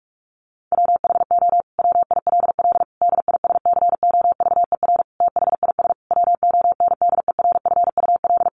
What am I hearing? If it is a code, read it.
Morse code, heard as a keyed tone, about 44 wpm, 700 Hz. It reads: WHO WILL DISCOVER THIS WONDERFUL
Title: Sentences from Top 1100 Words 45wpm